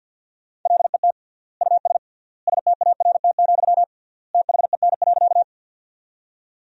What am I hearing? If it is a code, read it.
Morse code, heard as a keyed tone, about 49 wpm, 700 Hz. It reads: LET US START, THEN.
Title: 3x Word-Spacing - Sentences from Top 300 Words - 50wpm